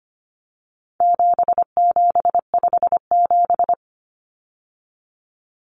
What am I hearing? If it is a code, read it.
Morse code, heard as a keyed tone, about 25 wpm, 700 Hz. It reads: 7757
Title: Numbers - 4-Digit 25wpm